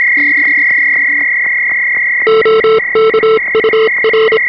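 Morse code, heard as a keyed tone, about 27 wpm, 440 Hz, with noise. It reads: OK UR
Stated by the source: 3.5 MHz CW received on an experimental direct digital HF receiver with RF band pass filters for the 3.5 and 7 MHz ham bands.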